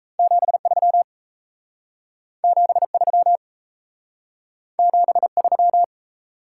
Morse code, heard as a keyed tone, about 38 wpm, 700 Hz. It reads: 73 73 73